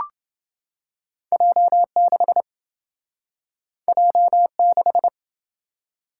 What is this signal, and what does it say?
Morse code, heard as a keyed tone, about 27 wpm, 700 Hz. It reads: J6 J6